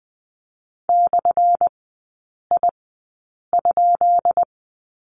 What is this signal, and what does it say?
Morse code, heard as a keyed tone, about 20 wpm, spaced as for 8 wpm, 700 Hz. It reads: /I?
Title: 178 Lesson 29e - Slash - Three Characters 20wpm